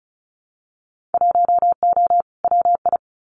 Morse code, heard as a keyed tone, about 34 wpm, 700 Hz. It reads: 1O WI
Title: Winter Field Day Exchanges Encore 35wpm